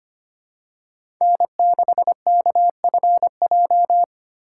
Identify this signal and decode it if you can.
Morse code, heard as a keyed tone, about 25 wpm, 700 Hz. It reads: N6KFJ
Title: USA Call Signs - 5 Characters - 4x Character Spacing - 25wpm